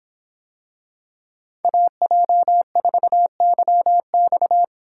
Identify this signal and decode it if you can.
Morse code, heard as a keyed tone, about 26 wpm, 700 Hz. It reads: AJ4YX